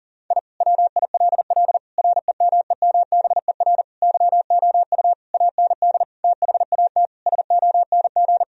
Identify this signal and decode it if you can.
Morse code, heard as a keyed tone, about 40 wpm, 700 Hz. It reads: I WILL REMEMBER YOU AND THAT SONG
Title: Sentences from Top 1000 Words - Review Part 2 - 40wpm